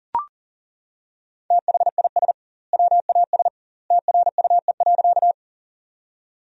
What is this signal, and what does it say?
Morse code, heard as a keyed tone, about 40 wpm, 700 Hz. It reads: THIS WAS TRUE.
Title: 2x Word-Spacing - Sentences from Top 400 Words - 40wpm